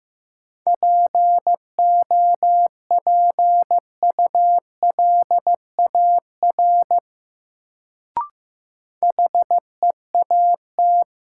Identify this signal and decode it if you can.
Morse code, heard as a keyed tone, about 15 wpm, 700 Hz. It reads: POPULAR HEAT
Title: Sentences from Top 1000 Words - Review Part 3 - New Word Drill - 15wpm